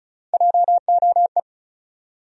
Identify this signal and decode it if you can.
Morse code, heard as a keyed tone, about 35 wpm, 700 Hz. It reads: JOE
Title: USA Common Names - Max 5 Letters - 35wpm